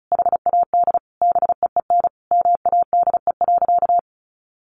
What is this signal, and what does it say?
Morse code, heard as a keyed tone, about 34 wpm, 700 Hz. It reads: HAD BEEN MADE.
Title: Sentences from Top 1400 Words 35wpm